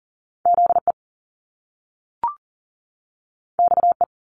Morse code, heard as a keyed tone, about 38 wpm, 700 Hz. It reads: ZE XE
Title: Top Two Letter Combinations in Words 40wpm